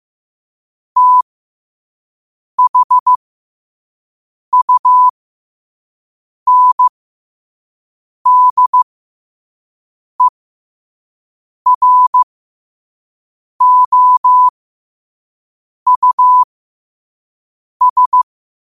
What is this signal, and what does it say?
Morse code, heard as a keyed tone, about 15 wpm, spaced as for 5 wpm, 1000 Hz. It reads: THUNDEROUS
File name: qotdfiles / 5wpm